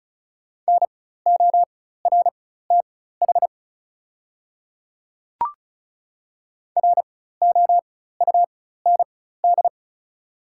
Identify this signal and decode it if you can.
Morse code, heard as a keyed tone, about 36 wpm, spaced as for 16 wpm, 700 Hz. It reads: NORTH ROUND
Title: 4x Character Spacing - Top 500 Words - 35wpm